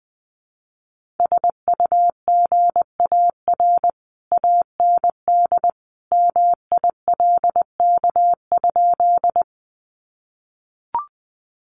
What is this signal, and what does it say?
Morse code, heard as a keyed tone, about 20 wpm, 700 Hz. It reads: SUGAR AND MILK?
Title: Sentences from Top 1500 Words 20wpm